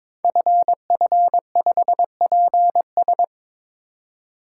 Morse code, heard as a keyed tone, about 22 wpm, 700 Hz. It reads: FF5PS
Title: Code Groups - 5 Letter:Numbers - 22wpm